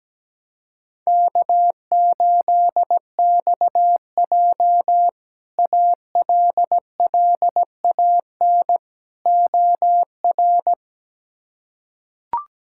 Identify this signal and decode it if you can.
Morse code, heard as a keyed tone, about 17 wpm, 700 Hz. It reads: K8XJ ALLAN OR
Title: K1USN SST QSO Practice - 17wpm